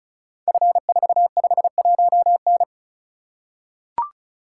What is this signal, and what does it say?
Morse code, heard as a keyed tone, about 36 wpm, 690 Hz. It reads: F451N
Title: Code Groups - 5 Letter:Numbers - 35wpm